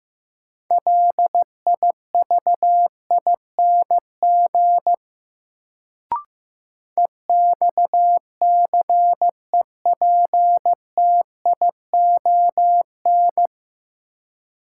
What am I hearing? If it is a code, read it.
Morse code, heard as a keyed tone, about 15 wpm, 700 Hz. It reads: LIVING EXCEPTION
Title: Sets of 4 Words - New Word Drill - 15wpm